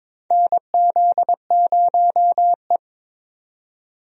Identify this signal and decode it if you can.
Morse code, heard as a keyed tone, about 22 wpm, 700 Hz. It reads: NZ0E
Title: USA Call Signs - 4 Characters - 22wpm